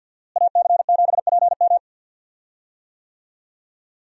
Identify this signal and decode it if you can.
Morse code, heard as a keyed tone, about 51 wpm, 700 Hz. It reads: AC7PM
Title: USA Call Signs - 5 Characters - Repeated 2x - 50wpm